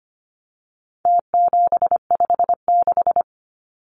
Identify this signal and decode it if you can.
Morse code, heard as a keyed tone, about 25 wpm, 700 Hz. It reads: T756
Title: Numbers - Contest Style - 4-Digit 25wpm